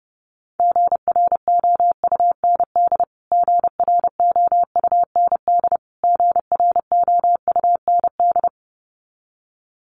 Morse code, heard as a keyed tone, about 30 wpm, 700 Hz. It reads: GROUND GROUND GROUND
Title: Top 500 Words Repeated 30wpm